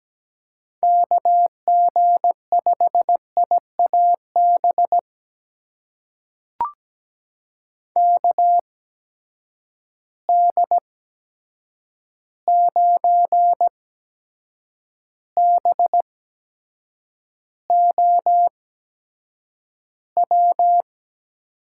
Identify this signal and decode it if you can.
Morse code, heard as a keyed tone, about 17 wpm, spaced as for 7 wpm, 700 Hz. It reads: KG5IAB KD9BOW
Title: USA Call Signs - 6 Characters - 8x Character Spacing - 17wpm